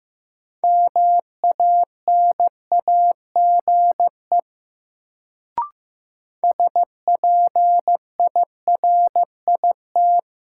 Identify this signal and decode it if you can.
Morse code, heard as a keyed tone, about 15 wpm, 700 Hz. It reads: MANAGE SPIRIT
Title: Sentences from Top 1400 Words - New Word Drill Alt - 15wpm